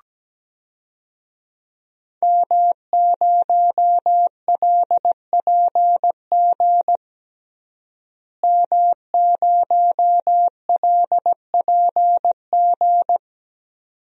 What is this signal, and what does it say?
Morse code, heard as a keyed tone, about 17 wpm, 700 Hz. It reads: M0LPG M0LPG
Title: DXCC-100 Call Signs - 5 Characters - Repeated 2x - 17wpm